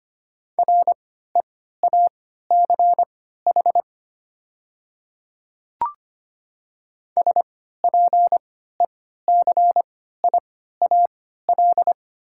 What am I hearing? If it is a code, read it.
Morse code, heard as a keyed tone, about 25 wpm, spaced as for 14 wpm, 700 Hz. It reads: REACH SPECIAL
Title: 3x Character Spacing - Top 500 Words - 25wpm